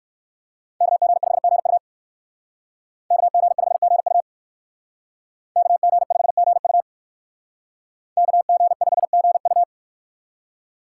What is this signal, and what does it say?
Morse code, heard as a keyed tone, about 58 wpm, 700 Hz. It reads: KG5GU KG5GU KG5GU KG5GU